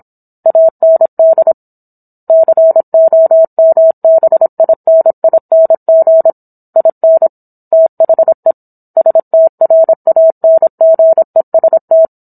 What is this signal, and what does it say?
Morse code, heard as a keyed tone, about 27 wpm, 650 Hz. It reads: AND COMBINING IN THE STRANGEST